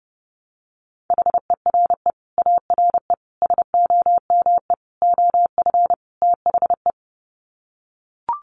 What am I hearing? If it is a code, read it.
Morse code, heard as a keyed tone, about 30 wpm, 700 Hz. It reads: HERE ARE SOME OF THE
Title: Sets of 5 Words Encore 30wpm